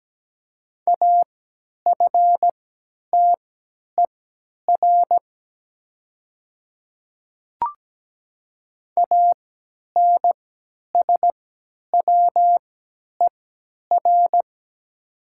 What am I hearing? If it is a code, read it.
Morse code, heard as a keyed tone, about 17 wpm, spaced as for 10 wpm, 700 Hz. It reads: AFTER ANSWER
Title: 3x Character Spacing - Top 500 Words - 17wpm